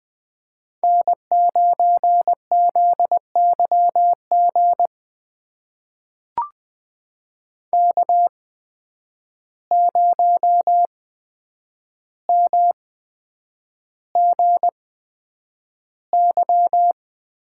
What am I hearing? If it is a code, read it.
Morse code, heard as a keyed tone, about 20 wpm, spaced as for 9 wpm, 700 Hz. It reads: N9ZYG K0MGY
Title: USA Call Signs - 5 Characters - 8x Character Spacing - 20wpm